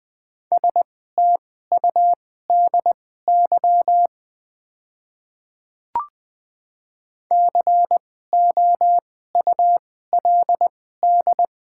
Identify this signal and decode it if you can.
Morse code, heard as a keyed tone, about 20 wpm, spaced as for 14 wpm, 700 Hz. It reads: STUDY COULD
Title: 2x Character Spacing - Top 500 Words - 20wpm